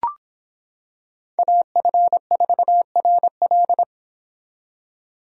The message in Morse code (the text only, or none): AF4RL